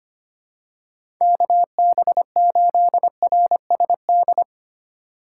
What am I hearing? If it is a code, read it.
Morse code, heard as a keyed tone, about 25 wpm, 700 Hz. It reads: KB8RSD
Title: USA Call Signs - 3x Character Spacing - 25wpm